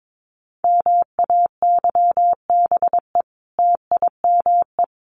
Morse code, heard as a keyed tone, about 22 wpm, 700 Hz. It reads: MAYBE TIME
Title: A Sound Of Thunder 22wpm